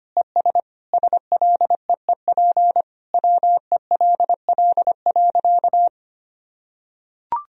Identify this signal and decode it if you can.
Morse code, heard as a keyed tone, about 25 wpm, 700 Hz. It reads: ES SLEEP WELL.
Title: Sentences from Top 500 Words and Abbreviations 25wpm